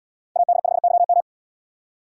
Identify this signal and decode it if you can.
Morse code, heard as a keyed tone, about 78 wpm, 700 Hz. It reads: AD5YM